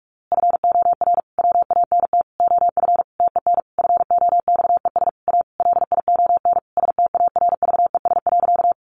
Morse code, heard as a keyed tone, about 44 wpm, 700 Hz. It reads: FOR WANT OF TEN FOXES A LION STARVES.
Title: A Sound Of Thunder 45wpm